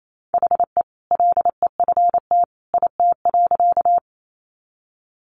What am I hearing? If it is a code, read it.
Morse code, heard as a keyed tone, about 28 wpm, 700 Hz. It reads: HE LEFT IT.
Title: A Sound Of Thunder 28wpm